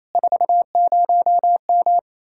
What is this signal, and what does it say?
Morse code, heard as a keyed tone, about 35 wpm, 700 Hz. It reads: 40M
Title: Speed-Racing Common QSO Phrases - 28wpm